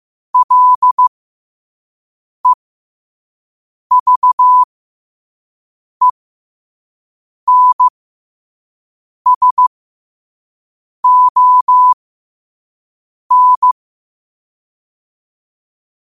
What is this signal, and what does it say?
Morse code, heard as a keyed tone, about 15 wpm, spaced as for 5 wpm, 1000 Hz. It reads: LEVENSON
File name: qotdfiles / 5wpm